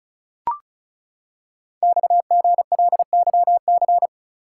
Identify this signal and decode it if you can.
Morse code, heard as a keyed tone, about 36 wpm, 690 Hz. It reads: XGLYC
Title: Code Groups - 5 Letters - 35wpm